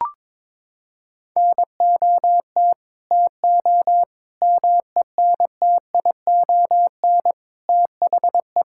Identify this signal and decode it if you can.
Morse code, heard as a keyed tone, about 22 wpm, 700 Hz. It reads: NOT TO MENTION THE
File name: practice / 22wpm / Sets of 6 Words - 22wpm